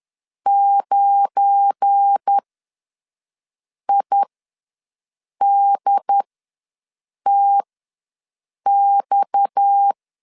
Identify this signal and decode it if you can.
Morse code, heard as a keyed tone, about 10 wpm, spaced as for 5 wpm, 800 Hz. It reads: 9IDTX
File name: LEZIONI / 9 2 ws 10wpm 4130